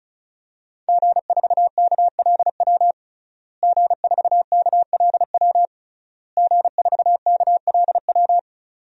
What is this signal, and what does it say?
Morse code, heard as a keyed tone, about 36 wpm, 700 Hz. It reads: G4KLW G4KLW G4KLW
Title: DXCC-100 Call Signs - 5 Characters - Repeated 3x - 35wpm